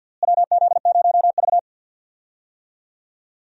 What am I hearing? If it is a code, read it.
Morse code, heard as a keyed tone, about 51 wpm, 700 Hz. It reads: WZ0V
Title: USA Call Signs - 4 Characters - Repeated 3x - 50wpm